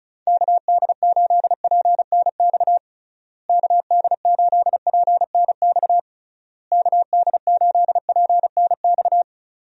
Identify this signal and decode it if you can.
Morse code, heard as a keyed tone, about 34 wpm, 700 Hz. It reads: KD8PNX KD8PNX KD8PNX
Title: USA Call Signs - 6 Characters - Repeated 3x - 35wpm